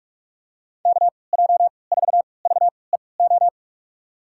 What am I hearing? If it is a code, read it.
Morse code, heard as a keyed tone, about 46 wpm, spaced as for 25 wpm, 700 Hz. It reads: KJ4VEO
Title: USA Call Signs - 3x Character Spacing - 45wpm